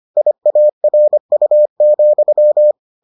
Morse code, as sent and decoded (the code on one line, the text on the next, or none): .. .- .-. ..- --..--
IARU,